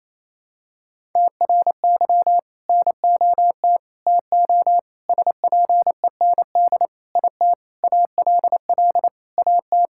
Morse code, heard as a keyed tone, about 28 wpm, 700 Hz. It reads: TRY NOT TO SPEND IT ALL AT
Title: Sentences from Top 1000 Words - Review Part 5 - 28wpm